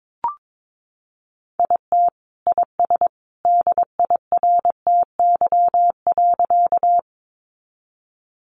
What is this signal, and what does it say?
Morse code, heard as a keyed tone, about 22 wpm, 700 Hz. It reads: IT IS DIRTY.
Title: Additional Easy Phrases 22wpm